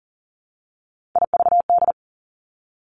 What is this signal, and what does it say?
Morse code, heard as a keyed tone, about 39 wpm, 700 Hz. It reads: IVD
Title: Three Challenging Letter-Number Combinations 40wpm